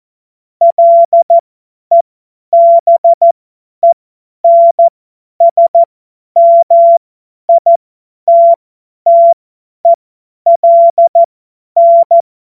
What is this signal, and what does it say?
Morse code, heard as a keyed tone, about 14 wpm, spaced as for 10 wpm, 680 Hz. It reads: LEBENSMITTELN